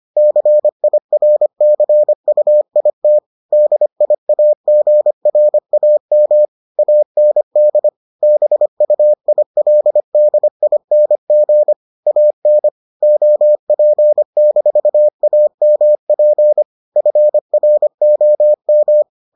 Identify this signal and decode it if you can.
Morse code, heard as a keyed tone, about 25 wpm, 600 Hz. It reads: CIRCUIT DIAGRAM AND BUILDING AN OP-AMP FROM